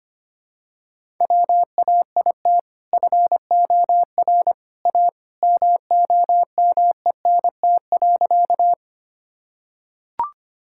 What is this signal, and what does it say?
Morse code, heard as a keyed tone, about 25 wpm, 700 Hz. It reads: WAIT FOR A MOMENT.
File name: practice / 25wpm / Sentences with 2-Word Phrases - 25wpm